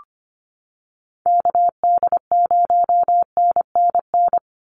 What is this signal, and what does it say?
Morse code, heard as a keyed tone, about 25 wpm, 700 Hz. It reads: KD0NNN
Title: USA Call Signs 25wpm